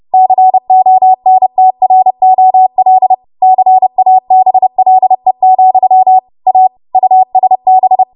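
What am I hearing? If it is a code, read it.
Morse code, heard as a keyed tone, about 30 wpm, 750 Hz. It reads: CONTROL CABLE, A USB
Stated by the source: Morse practice transmission at 30 WPM.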